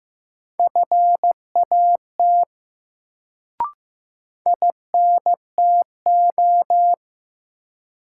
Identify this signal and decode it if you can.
Morse code, heard as a keyed tone, about 15 wpm, 700 Hz. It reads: FAT INTO
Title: Sentences from Top 600 Words - New Word Drill - 15wpm